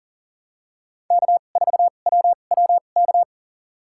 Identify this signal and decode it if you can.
Morse code, heard as a keyed tone, about 40 wpm, spaced as for 29 wpm, 700 Hz. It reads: K4WWK
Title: USA Call Signs - 5 Characters - 2x Character Spacing - 40wpm